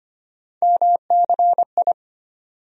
Morse code, heard as a keyed tone, about 25 wpm, 700 Hz. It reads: MCI
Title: USA Call Sign Postfixes - 3 Letters 25wpm